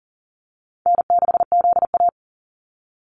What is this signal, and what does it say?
Morse code, heard as a keyed tone, about 40 wpm, 700 Hz. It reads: N6ZA
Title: NA SOTA Activator Callsigns 40wpm